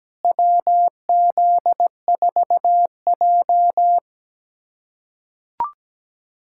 Morse code, heard as a keyed tone, about 17 wpm, 700 Hz. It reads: WZ4J
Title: USA Callsign DE Callsign - 4 Characters - 17wpm